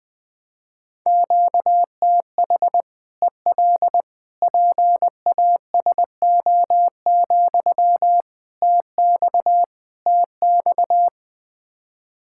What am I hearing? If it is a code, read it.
Morse code, heard as a keyed tone, about 20 wpm, 700 Hz. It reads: QTH EL PASO, TX TX
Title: Practice QSO - Exchange 1 - Beginning - 20wpm